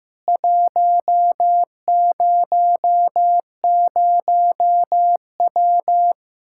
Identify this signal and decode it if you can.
Morse code, heard as a keyed tone, about 15 wpm, 700 Hz. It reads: 100W
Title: Practice QSO - Exchange 2 - Info - 15wpm